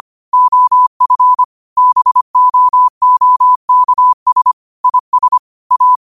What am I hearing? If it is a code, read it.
Morse code, heard as a keyed tone, about 25 wpm, 1000 Hz. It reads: OF BOOKS IS A
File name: qotdfiles / 25wpm